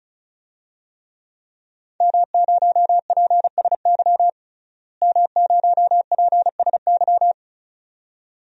Morse code, heard as a keyed tone, about 36 wpm, 700 Hz. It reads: M0PSY M0PSY
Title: DXCC-100 Call Signs - 5 Characters - Repeated 2x - 35wpm